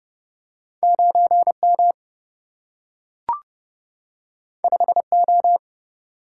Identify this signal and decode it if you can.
Morse code, heard as a keyed tone, about 30 wpm, 700 Hz. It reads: 9 M 5 O
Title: Single-Letter Single-Number 30wpm